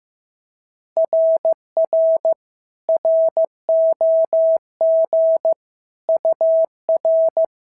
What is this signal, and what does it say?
Morse code, heard as a keyed tone, about 14 wpm, 650 Hz. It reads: RR ROG UR
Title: SOTA QSO Practice 15wpm